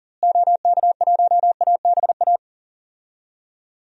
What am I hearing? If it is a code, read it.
Morse code, heard as a keyed tone, about 40 wpm, 700 Hz. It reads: OK1ABA
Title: DXCC-100 Call Signs - 6 Characters - Repeated 3x - 40wpm